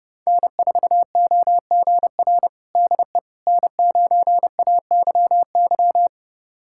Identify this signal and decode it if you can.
Morse code, heard as a keyed tone, about 30 wpm, 700 Hz. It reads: N4OGR DE N9AYY
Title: USA Callsign DE Callsign - 5 Characters - 30wpm